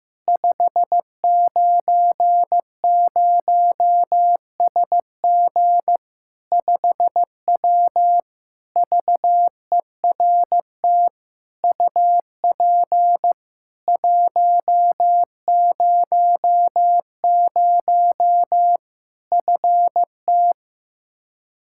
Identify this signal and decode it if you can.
Morse code, heard as a keyed tone, about 16 wpm, 700 Hz. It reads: 590SG 5W VERT UP 100 FT
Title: Practice QSO - Exchange 2 - Info - 15wpm